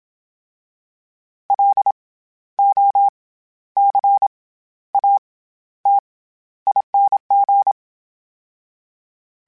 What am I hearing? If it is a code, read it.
Morse code, heard as a keyed tone, about 27 wpm, 800 Hz. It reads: L O C A T ING
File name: Special INGs-mp3s / Special INGs-mp3s / SlowToFaster ing @25